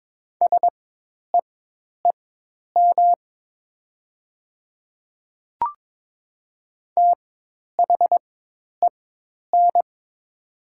Morse code, heard as a keyed tone, about 22 wpm, spaced as for 10 wpm, 700 Hz. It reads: SEEM THEN
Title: 4x Character Spacing - Top 500 Words - 22wpm